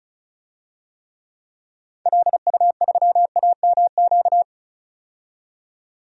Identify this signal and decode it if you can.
Morse code, heard as a keyed tone, about 36 wpm, 700 Hz. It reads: LU3AMQ
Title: DXCC-100 Call Signs - 6 Characters - 35wpm